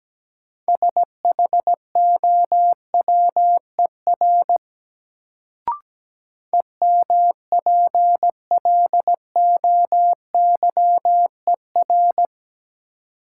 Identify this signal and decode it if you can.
Morse code, heard as a keyed tone, about 17 wpm, 700 Hz. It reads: SHOWER EMPLOYER
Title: Sentences from Top 1600 Words - New Word Drill Alt - 17wpm